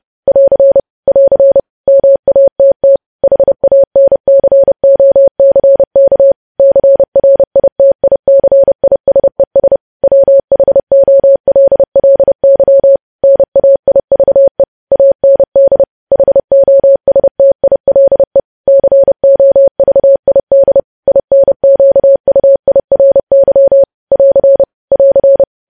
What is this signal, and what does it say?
Morse code, heard as a keyed tone, about 30 wpm, 550 Hz. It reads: <AR> <AR> MATT HANCOCK CRITICISES WHOLLY NAIVE AND HOSTILE COVID INQUIRY <AR> <AR>
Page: news headlines in morse code at 30 wpm